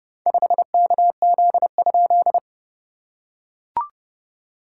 Morse code, heard as a keyed tone, about 30 wpm, 690 Hz. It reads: 5KZ?
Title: SOTA QSO Practice 30wpm